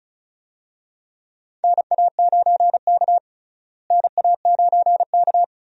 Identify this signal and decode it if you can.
Morse code, heard as a keyed tone, about 34 wpm, 700 Hz. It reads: NA9K NA9K
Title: USA Call Signs - 4 Characters - Repeated 2x - 35wpm